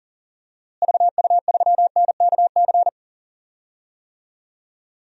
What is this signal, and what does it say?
Morse code, heard as a keyed tone, about 40 wpm, 700 Hz. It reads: VU3NKC
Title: DXCC-100 Call Signs - 6 Characters - Repeated 3x - 40wpm